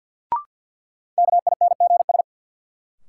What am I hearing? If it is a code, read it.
Morse code, heard as a keyed tone, about 48 wpm, 700 Hz. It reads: KINGS
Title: Sentences from Top 1000 Words - Review Part 6 - New Word Drill - 50wpm